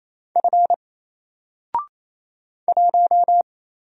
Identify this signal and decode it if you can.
Morse code, heard as a keyed tone, about 28 wpm, 700 Hz. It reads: F 1
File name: Single-Letter-Number-28wpm